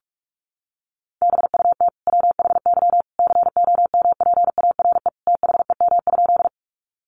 Morse code, heard as a keyed tone, about 44 wpm, 700 Hz. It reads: BUT WHY COMPARE THEM?
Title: Sentences from Top 1300 Words 45wpm